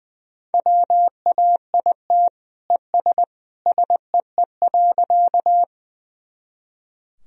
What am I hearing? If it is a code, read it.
Morse code, heard as a keyed tone, about 21 wpm, 700 Hz. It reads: WAIT ES SEE.
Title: Sentences from Top 500 Words and Abbreviations 20wpm